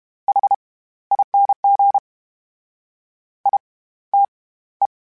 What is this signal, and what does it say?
Morse code, heard as a keyed tone, about 32 wpm, spaced as for 17 wpm, 800 Hz. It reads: HING ITE
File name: Special INGs / Special INGs-mp3s / SlowToFaster ing @30